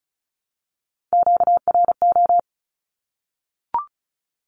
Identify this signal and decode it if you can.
Morse code, heard as a keyed tone, about 34 wpm, 700 Hz. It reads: QRO
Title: Q-Codes 35wpm